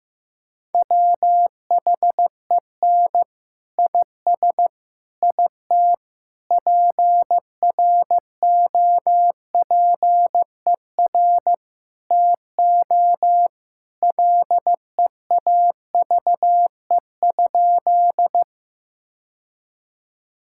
Morse code, heard as a keyed tone, about 15 wpm, 700 Hz. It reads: WHEN IS IT PROPER TO LEAVE?
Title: Sentences from Top 1000 Words - Review Part 5 - 15wpm